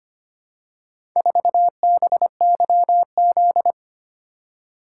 Morse code, heard as a keyed tone, about 25 wpm, 700 Hz. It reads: 4BYZ
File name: practice / 25wpm / Four Letter-Number Combinations - 25wpm